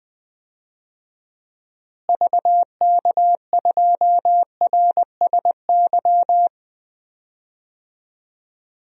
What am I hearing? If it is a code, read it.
Morse code, heard as a keyed tone, about 20 wpm, 700 Hz. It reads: VK2RSY
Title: DXCC-100 Call Signs - 6 Characters - Repeated 2x - 20wpm